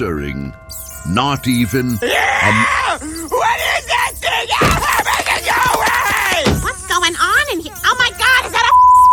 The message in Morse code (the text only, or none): none